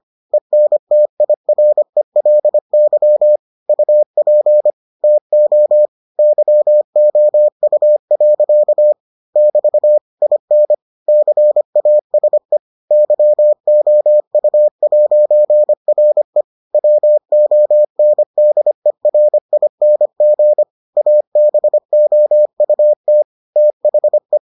Morse code, heard as a keyed tone, about 25 wpm, 600 Hz. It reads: ENTIRELY UP TO YOU. <BT> IN CASE YOU'RE WONDERING ABOUT THE